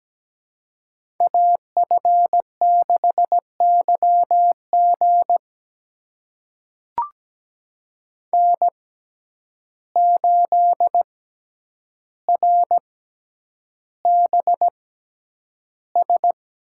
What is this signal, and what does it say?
Morse code, heard as a keyed tone, about 17 wpm, spaced as for 9 wpm, 700 Hz. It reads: AF6YG N8RBS
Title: USA Call Signs - 5 Characters - 6x Character Spacing - 17wpm